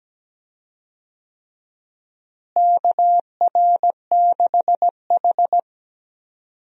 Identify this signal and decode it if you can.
Morse code, heard as a keyed tone, about 17 wpm, 700 Hz. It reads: KR6H
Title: USA Call Signs - 4 Characters - Repeated 3x - 17wpm